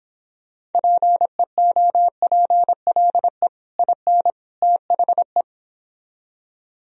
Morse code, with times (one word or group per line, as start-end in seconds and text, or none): 0.7-3.5 PEOPLE
3.8-4.3 IN
4.6-5.4 THE